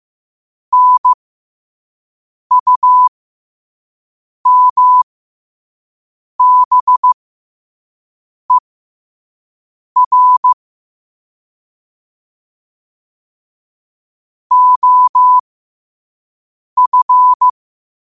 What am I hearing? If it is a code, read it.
Morse code, heard as a keyed tone, about 15 wpm, spaced as for 5 wpm, 1000 Hz. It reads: NUMBER OF